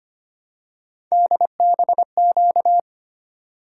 Morse code, heard as a keyed tone, about 25 wpm, 700 Hz. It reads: DBQ
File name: Three Letter-Number Combinations - 25wpm